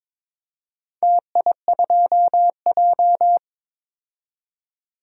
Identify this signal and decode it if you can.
Morse code, heard as a keyed tone, about 22 wpm, 700 Hz. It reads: TI2J
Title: DXCC-100 Call Signs - 4 Characters - 22wpm